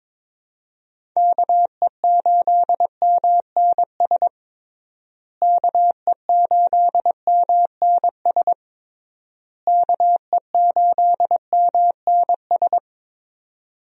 Morse code, heard as a keyed tone, about 22 wpm, 700 Hz. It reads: KE8MNS KE8MNS KE8MNS
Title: USA Call Signs - 6 Characters - Repeated 3x - 22wpm